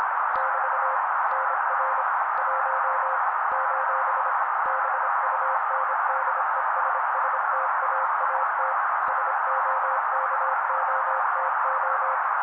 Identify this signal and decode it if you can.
Morse code, heard as a keyed tone, about 26 wpm, 530 Hz. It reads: <BT> NR 1 : BUNDESSTAAT SOKOTO
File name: daily / 25wpm noise